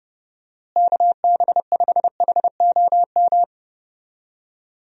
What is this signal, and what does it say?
Morse code, heard as a keyed tone, about 30 wpm, 700 Hz. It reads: KB5HOM
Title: USA Call Signs - 6 Characters - Repeated 3x - 30wpm